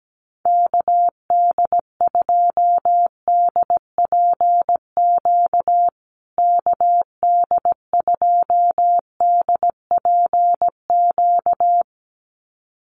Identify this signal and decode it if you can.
Morse code, heard as a keyed tone, about 17 wpm, 700 Hz. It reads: KD2DPQ KD2DPQ
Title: USA Call Signs Repeated 2x 17wpm